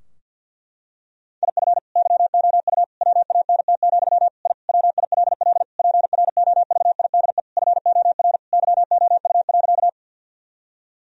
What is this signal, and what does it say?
Morse code, heard as a keyed tone, about 50 wpm, 700 Hz. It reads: IF YOU WANT, I WILL PROVIDE FOR YOU.
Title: Sentences from Top 1000 Words - Review Part 3 - 50wpm